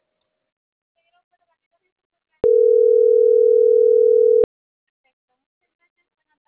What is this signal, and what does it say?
Ringback tone (North America), 440 + 480 Hz.